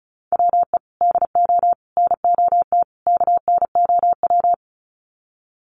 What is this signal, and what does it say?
Morse code, heard as a keyed tone, about 34 wpm, 700 Hz. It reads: WE DO NOT KNOW
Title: Sets of 4 Words 35wpm